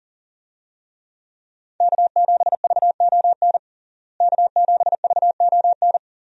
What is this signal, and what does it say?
Morse code, heard as a keyed tone, about 40 wpm, 700 Hz. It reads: K7VON K7VON
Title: USA Call Signs - 5 Characters - Repeated 2x - 40wpm